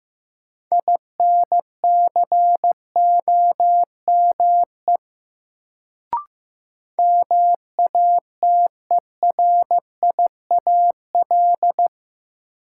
Morse code, heard as a keyed tone, about 15 wpm, 700 Hz. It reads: INCOME MATERIAL
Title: Sentences from Top 1000 Words - Review Part 5 - New Word Drill - 15wpm